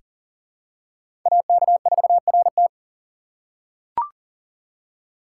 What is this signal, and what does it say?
Morse code, heard as a keyed tone, about 40 wpm, 700 Hz. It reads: AK4RT
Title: USA Call Signs - 2x Character Spacing - 40wpm